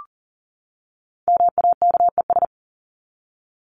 Morse code, heard as a keyed tone, about 39 wpm, 700 Hz. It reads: MAKES
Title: ES:S Suffix - Top 100 Words with Base Words 40wpm